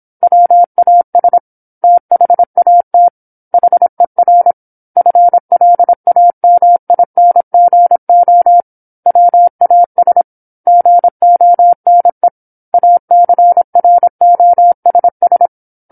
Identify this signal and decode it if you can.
Morse code, heard as a keyed tone, about 27 wpm, 700 Hz. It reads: WAS THAT HER FLAMINGO WAS GONE ACROSS
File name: 08-Chapter-08 26 WPM